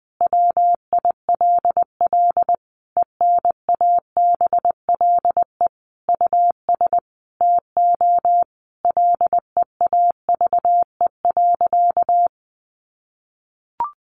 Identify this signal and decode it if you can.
Morse code, heard as a keyed tone, about 19 wpm, 700 Hz. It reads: WILL ENABLE US TO LEAVE.
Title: Sentences from Top 1500 Words 20wpm